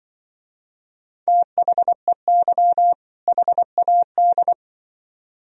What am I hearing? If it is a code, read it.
Morse code, heard as a keyed tone, about 23 wpm, 700 Hz. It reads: THEY HAD
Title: Speed-Racing - Sets of 2 Words 20wpm